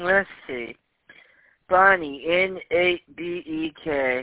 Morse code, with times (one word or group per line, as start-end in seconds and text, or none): none